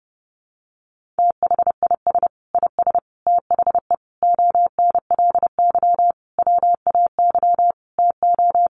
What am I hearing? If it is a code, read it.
Morse code, heard as a keyed tone, about 30 wpm, 700 Hz. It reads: THIS IS THE ONLY WAY TO
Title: Sentences from Top 1400 Words 30wpm